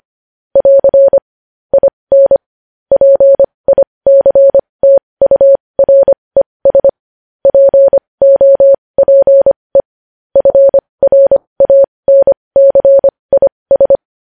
Morse code, heard as a keyed tone, about 25 wpm, spaced as for 20 wpm, 550 Hz. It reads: <AR> IN PICTURES POPE FRANCIS
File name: podcast / News headlines-1x-25x20